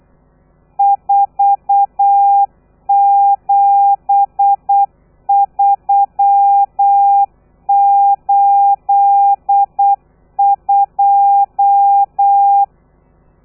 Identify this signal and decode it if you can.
Morse code, heard as a keyed tone, about 8 wpm, 800 Hz. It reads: 47382
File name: telegrafia / i2tfj / Lez13 35